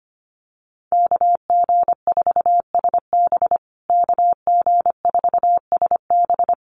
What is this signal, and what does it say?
Morse code, heard as a keyed tone, about 25 wpm, 700 Hz. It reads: KG4SB KG4SB
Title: USA Call Signs Repeated 3x 25wpm